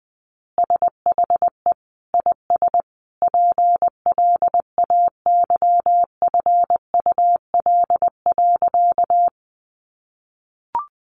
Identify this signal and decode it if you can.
Morse code, heard as a keyed tone, about 20 wpm, 700 Hz. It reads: SHE IS PLAYFUL.
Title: Easy Phrases 20wpm